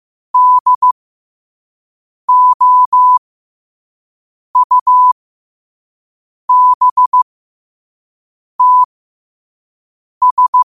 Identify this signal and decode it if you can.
Morse code, heard as a keyed tone, about 15 wpm, spaced as for 5 wpm, 1000 Hz. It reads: DOUBTS
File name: qotdfiles / 5wpm